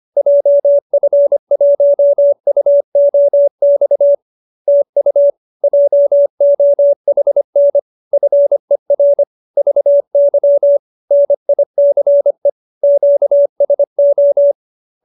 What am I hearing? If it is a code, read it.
Morse code, heard as a keyed tone, about 25 wpm, 570 Hz. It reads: JF1UOX TU JOHN FER VY NICE QSO